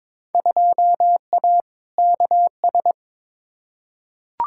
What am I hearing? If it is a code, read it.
Morse code, heard as a keyed tone, about 22 wpm, 700 Hz. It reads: 2A KS
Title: Field Day Exchanges - 22wpm